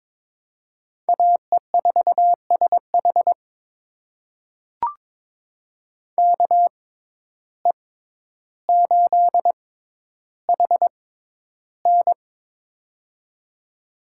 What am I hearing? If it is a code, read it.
Morse code, heard as a keyed tone, about 22 wpm, spaced as for 11 wpm, 700 Hz. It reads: AE4SH KE8HN
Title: USA Call Signs - 5 Characters - 6x Character Spacing - 22wpm